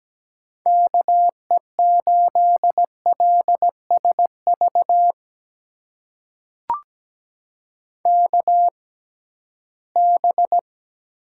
Morse code, heard as a keyed tone, about 17 wpm, spaced as for 13 wpm, 700 Hz. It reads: KE8LSV KB
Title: USA Call Signs - 6 Characters - 6x Character Spacing - 17wpm